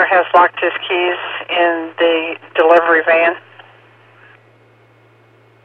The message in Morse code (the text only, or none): none